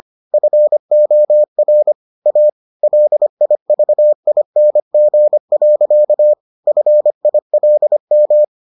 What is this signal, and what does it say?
Morse code, heard as a keyed tone, about 25 wpm, 600 Hz. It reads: FOR A LIVING. FILM